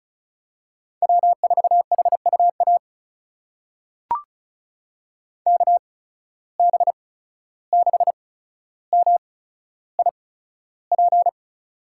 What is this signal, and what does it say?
Morse code, heard as a keyed tone, about 36 wpm, spaced as for 14 wpm, 700 Hz. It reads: W4HUA KB6MIP
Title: USA Call Signs - 8x Character Spacing - 35wpm